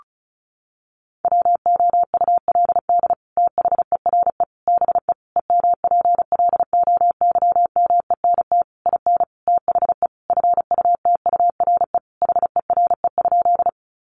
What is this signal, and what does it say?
Morse code, heard as a keyed tone, about 34 wpm, 700 Hz. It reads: WOULD THERE BE EMPLOYMENT IN THE FUTURE HERE?
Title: Sentences from Top 1500 Words 35wpm